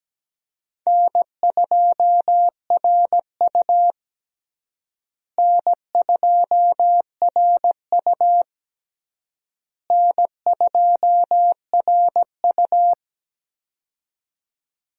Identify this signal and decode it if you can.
Morse code, heard as a keyed tone, about 17 wpm, 700 Hz. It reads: N2RU N2RU N2RU
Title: USA Call Signs - 4 Characters - Repeated 3x - 17wpm